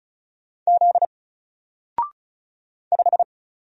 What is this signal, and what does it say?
Morse code, heard as a keyed tone, about 33 wpm, 700 Hz. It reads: Z 5
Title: Single Letter-Number 35wpm